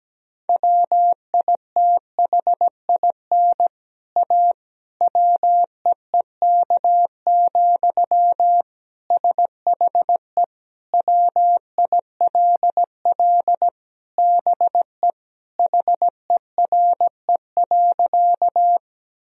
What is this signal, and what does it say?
Morse code, heard as a keyed tone, about 17 wpm, 700 Hz. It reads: WITHIN A WEEK, SHE WILL BE HERE.
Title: Sentences from Top 1000 Words - Review Part 4 - 17wpm